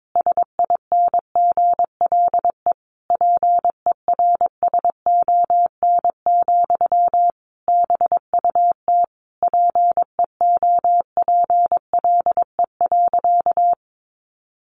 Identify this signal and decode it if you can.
Morse code, heard as a keyed tone, about 22 wpm, 700 Hz. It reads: SINGLE PERSON, BUT PEOPLE.
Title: Sentences from Top 1400 Words 22wpm